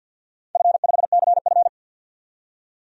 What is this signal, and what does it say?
Morse code, heard as a keyed tone, about 52 wpm, 700 Hz. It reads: F5CF
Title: DXCC-100 Call Signs - 4 Characters - 50wpm